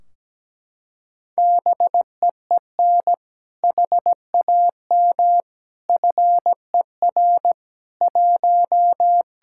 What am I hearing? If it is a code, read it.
Morse code, heard as a keyed tone, about 17 wpm, 700 Hz. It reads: BEEN HAM FER 1
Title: Practice QSO - Exchange 3 - More Info - 17wpm